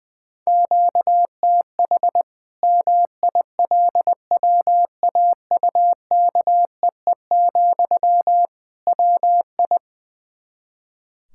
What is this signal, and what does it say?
Morse code, heard as a keyed tone, about 20 wpm, 700 Hz. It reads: QTH MILWAUKEE, WI
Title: Practice QSO - Exchange 1 - Beginning - 20wpm